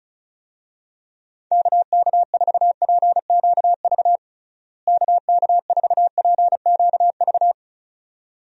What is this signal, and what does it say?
Morse code, heard as a keyed tone, about 36 wpm, 700 Hz. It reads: KK4PQV KK4PQV
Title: USA Call Signs - 6 Characters - Repeated 2x - 35wpm